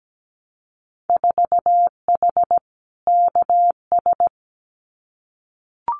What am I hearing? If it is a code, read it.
Morse code, heard as a keyed tone, about 17 wpm, 700 Hz. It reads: 4H KS
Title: Winter Field Day Exchanges Encore 17wpm